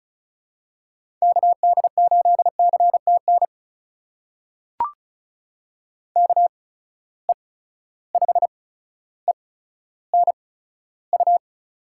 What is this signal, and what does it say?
Morse code, heard as a keyed tone, about 36 wpm, spaced as for 15 wpm, 700 Hz. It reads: KD8CTN KE5ENU
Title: USA Call Signs - 6 Characters - 8x Character Spacing - 35wpm